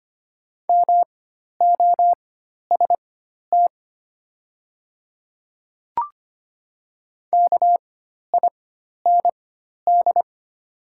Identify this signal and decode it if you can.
Morse code, heard as a keyed tone, about 25 wpm, spaced as for 12 wpm, 700 Hz. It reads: MOST KIND
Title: 4x Character Spacing - Top 200 Words - 25wpm